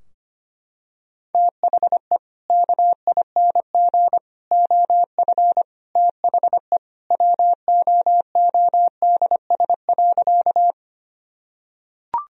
Morse code, heard as a keyed tone, about 25 wpm, 700 Hz. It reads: THE KING OF THE WOODS.
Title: Sentences from Top 1000 Words - Review Part 2 - 25wpm